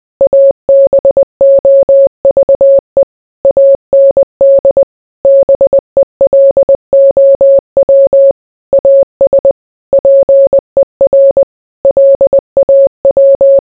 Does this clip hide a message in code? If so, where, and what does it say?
Yes: Morse code from 0.2 to 13.6 s, heard as a keyed tone, about 20 wpm, 550 Hz. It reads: ABOVE AND BELOW AS PER LAW